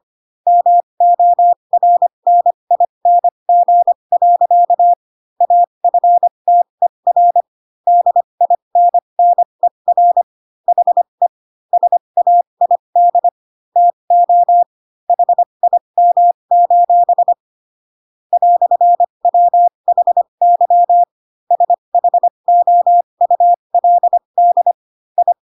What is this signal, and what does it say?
Morse code, heard as a keyed tone, about 25 wpm, spaced as for 22 wpm, 700 Hz. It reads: MORNING. AFTER DINNER HE SAID TO HIM: "WHY SHOULD I
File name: Pinocchio / Pinocchio ch8 18 25 ik1hge